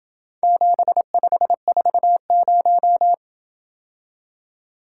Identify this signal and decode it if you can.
Morse code, heard as a keyed tone, about 27 wpm, 700 Hz. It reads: 7540